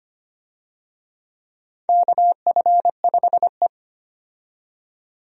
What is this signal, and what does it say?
Morse code, heard as a keyed tone, about 25 wpm, 700 Hz. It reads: KF5E
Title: USA Call Signs - 4 Characters - 25wpm